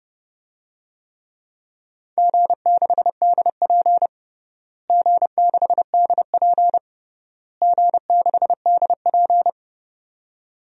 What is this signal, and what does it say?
Morse code, heard as a keyed tone, about 30 wpm, 700 Hz. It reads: G6DP G6DP G6DP
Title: DXCC-100 Call Signs - 4 Characters - Repeated 3x - 30wpm